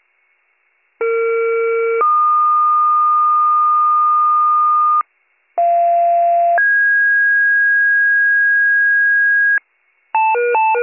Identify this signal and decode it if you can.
Two-tone page, heard as a one-second tone, then a three-second tone, about 460 then 1220 Hz, about 690 then 1650 Hz, with noise. It is heard from 1.0 to 9.6 s.